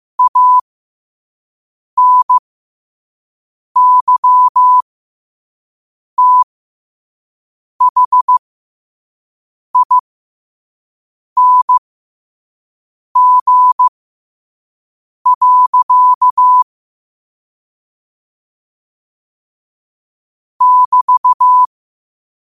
Morse code, heard as a keyed tone, about 15 wpm, spaced as for 5 wpm, 1000 Hz. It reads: ANYTHING. <BT>
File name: qotdfiles / 5wpm